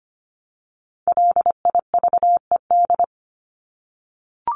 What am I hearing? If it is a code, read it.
Morse code, heard as a keyed tone, about 25 wpm, 700 Hz. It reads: LIVED
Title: ED Suffix - Top 100 Words with Base Words 25wpm